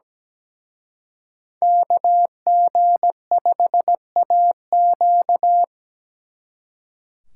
Morse code, heard as a keyed tone, about 20 wpm, 700 Hz. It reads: KG5AQ
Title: Speed Racing - USA 2x2 Call Signs 17wpm